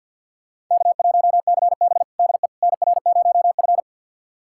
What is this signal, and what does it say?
Morse code, heard as a keyed tone, about 50 wpm, 700 Hz. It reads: K1CB DE NR0F
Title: USA Callsign DE Callsign - 4 Characters - 50wpm